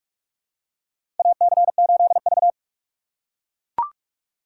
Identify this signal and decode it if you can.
Morse code, heard as a keyed tone, about 43 wpm, 700 Hz. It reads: AC8V
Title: USA Call Signs - 4 Characters - 2x Character Spacing - 45wpm